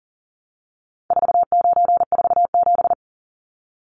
Morse code, heard as a keyed tone, about 40 wpm, 700 Hz. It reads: 4947
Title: Numbers - 4-Digit 40wpm